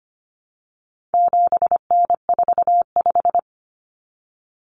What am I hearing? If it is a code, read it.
Morse code, heard as a keyed tone, about 25 wpm, 700 Hz. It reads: 7N45
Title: Common Cut Numbers - 4-Digit 25wpm